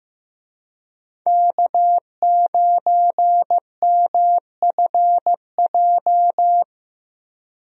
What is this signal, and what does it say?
Morse code, heard as a keyed tone, about 15 wpm, 700 Hz. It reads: K9MFJ
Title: USA Call Signs - 5 Characters - Repeated 3x - 15wpm